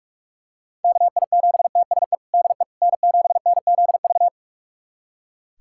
Morse code, heard as a keyed tone, about 41 wpm, 700 Hz. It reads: KI7TSE DE N7NZV
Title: USA Callsign DE Callsign 45wpm